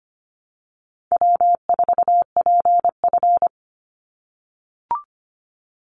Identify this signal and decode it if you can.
Morse code, heard as a keyed tone, about 25 wpm, 700 Hz. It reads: W4PF
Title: Easy Call Signs - 25wpm